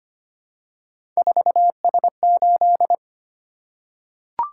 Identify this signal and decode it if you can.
Morse code, heard as a keyed tone, about 25 wpm, 700 Hz. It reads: 4S8
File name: Three Letter-Number Combinations - 25wpm